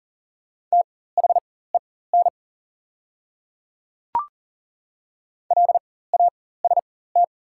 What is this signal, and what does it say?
Morse code, heard as a keyed tone, about 40 wpm, spaced as for 19 wpm, 700 Hz. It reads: THEN LAST
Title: 4x Character Spacing - Top 300 Words - 40wpm